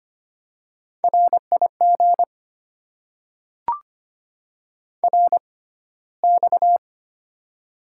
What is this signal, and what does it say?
Morse code, heard as a keyed tone, about 25 wpm, spaced as for 15 wpm, 700 Hz. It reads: RIG RX
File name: Top 90 QSO Elements - 6x Character Spacing - 25wpm